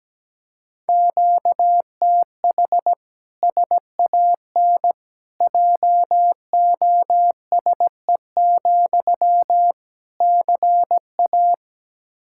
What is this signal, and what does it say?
Morse code, heard as a keyed tone, about 17 wpm, 700 Hz. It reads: QTH SAN JOSE, CA
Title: Practice QSO - Exchange 1 - Beginning - 17wpm